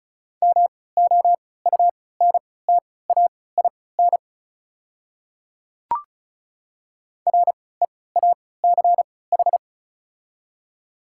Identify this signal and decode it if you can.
Morse code, heard as a keyed tone, about 36 wpm, spaced as for 20 wpm, 690 Hz. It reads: MOUNTAIN REACH
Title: 3x Character Spacing - Top 500 Words - 35wpm